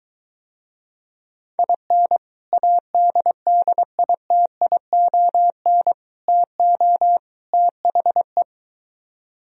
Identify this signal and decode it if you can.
Morse code, heard as a keyed tone, about 23 wpm, 700 Hz. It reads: IN ADDITION TO THE